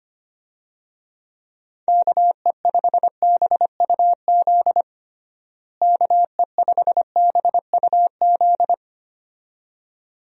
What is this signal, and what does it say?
Morse code, heard as a keyed tone, about 25 wpm, 700 Hz. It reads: KE5BUZ KE5BUZ
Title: USA Call Signs - 6 Characters - Repeated 2x - 25wpm